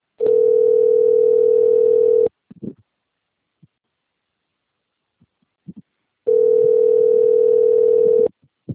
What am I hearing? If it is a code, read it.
Ringback tone (North America), 440 + 480 Hz.